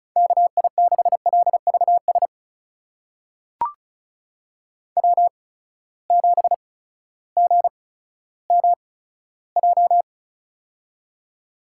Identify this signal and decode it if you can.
Morse code, heard as a keyed tone, about 36 wpm, spaced as for 16 wpm, 700 Hz. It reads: KI6LVS W7GMJ
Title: USA Call Signs - 8x Character Spacing - 35wpm